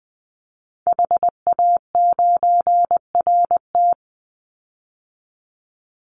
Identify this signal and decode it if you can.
Morse code, heard as a keyed tone, about 20 wpm, 700 Hz. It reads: HA9RT
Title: Call Signs Repeated 2x - 20wpm